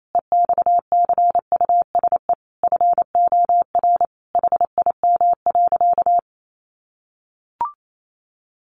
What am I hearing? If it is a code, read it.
Morse code, heard as a keyed tone, about 28 wpm, 700 Hz. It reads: EXCUSE FOR HIM.
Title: Sentences from Top 700 Words 28wpm